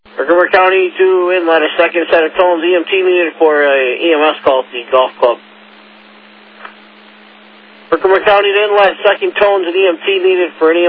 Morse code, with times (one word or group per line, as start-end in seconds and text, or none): none